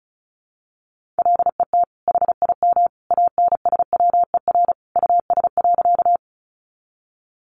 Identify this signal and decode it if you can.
Morse code, heard as a keyed tone, about 34 wpm, 700 Hz. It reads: LET HIM ANSWER US.
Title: Sentences from Top 300 Words 35wpm